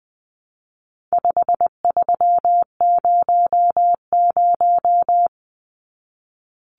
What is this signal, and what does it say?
Morse code, heard as a keyed tone, about 20 wpm, 700 Hz. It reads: 5300
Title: Numbers - 4-Digit 20wpm